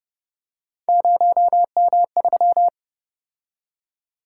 Morse code, heard as a keyed tone, about 29 wpm, 700 Hz. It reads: 0M3